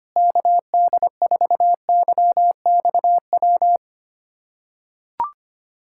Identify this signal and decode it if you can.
Morse code, heard as a keyed tone, about 25 wpm, 700 Hz. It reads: KD4YXW
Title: USA Callsign DE Callsign 25wpm